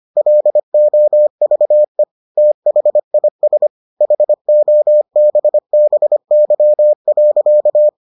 Morse code, heard as a keyed tone, about 25 wpm, 600 Hz. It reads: LOVE THIS HOBBY.